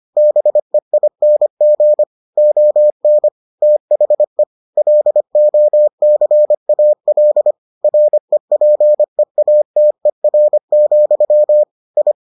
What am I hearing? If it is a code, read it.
Morse code, heard as a keyed tone, about 25 wpm, 600 Hz. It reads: BEING ON THE LOCAL REPEATER, I